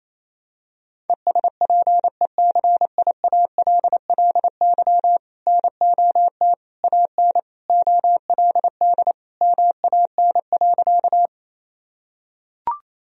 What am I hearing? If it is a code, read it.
Morse code, heard as a keyed tone, about 28 wpm, 700 Hz. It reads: ESPECIALLY NOT AN OLD MAN.
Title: Sentences from Top 1000 Words - Review Part 4 - 28wpm